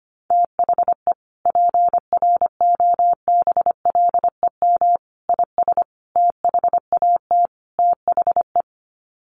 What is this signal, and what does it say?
Morse code, heard as a keyed tone, about 25 wpm, 700 Hz. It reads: THE PROBLEM IS THAT THE
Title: Sets of 5 Words Encore2 25wpm